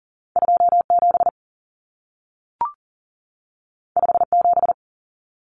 Morse code, heard as a keyed tone, about 40 wpm, 700 Hz. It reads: 27 57
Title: Numbers - 2-Digit 40wpm